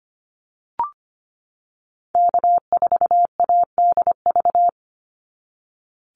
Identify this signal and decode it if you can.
Morse code, heard as a keyed tone, about 25 wpm, 690 Hz. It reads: K4ADV
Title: NA SOTA Activator Callsigns 25wpm